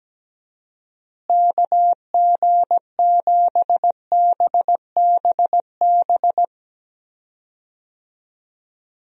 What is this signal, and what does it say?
Morse code, heard as a keyed tone, about 17 wpm, 700 Hz. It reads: KG7BBB
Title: USA Call Signs - 6 Characters - Repeated 2x - 17wpm